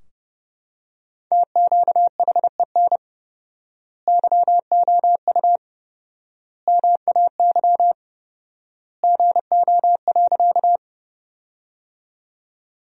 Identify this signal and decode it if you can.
Morse code, heard as a keyed tone, about 30 wpm, 700 Hz. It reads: TQHEN YOU MAY GO.
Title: 4x Word-Spacing - Sentences from Top 100 Words - 30wpm